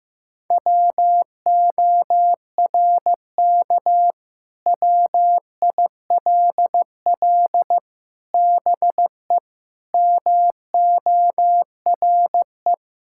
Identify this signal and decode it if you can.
Morse code, heard as a keyed tone, about 15 wpm, 700 Hz. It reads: WORK WILL BE MORE
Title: Sentences from Top 1000 Words - Review Part 6 - 15wpm